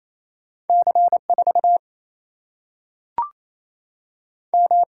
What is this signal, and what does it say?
Morse code, heard as a keyed tone, about 28 wpm, 700 Hz. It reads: C4 M
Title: Single-Letter Single-Number 28wpm